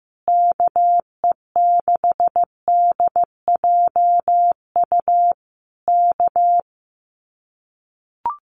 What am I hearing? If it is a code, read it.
Morse code, heard as a keyed tone, about 15 wpm, 700 Hz. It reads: KE6DJU K
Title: Common QSO Phrases Encore 15wpm